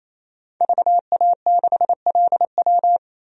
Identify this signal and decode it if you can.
Morse code, heard as a keyed tone, about 28 wpm, 700 Hz. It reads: VA6LW
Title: DXCC-100 Call Signs - 5 Characters - 28wpm